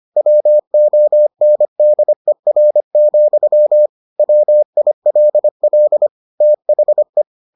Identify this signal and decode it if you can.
Morse code, heard as a keyed tone, about 25 wpm, 600 Hz. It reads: WONDER, WILL THE